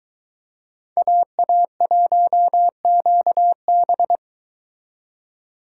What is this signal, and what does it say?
Morse code, heard as a keyed tone, about 22 wpm, 700 Hz. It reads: AA1QB